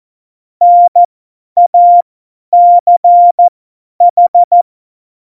Morse code, heard as a keyed tone, about 14 wpm, spaced as for 10 wpm, 700 Hz. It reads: NACH
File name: Morsekurs / Witze / Witz 04. Juni 2019, 50 F70